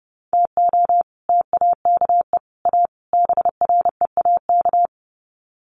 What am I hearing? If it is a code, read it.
Morse code, heard as a keyed tone, about 30 wpm, 700 Hz. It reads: TO TAKE A BREAK
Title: Sets of 4 Words 30wpm